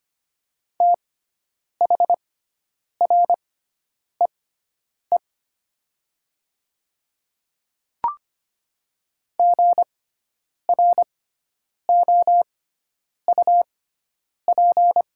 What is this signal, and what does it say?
Morse code, heard as a keyed tone, about 25 wpm, spaced as for 9 wpm, 700 Hz. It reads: THREE GROUP
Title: 6x Character Spacing - Top 300 Words - 25wpm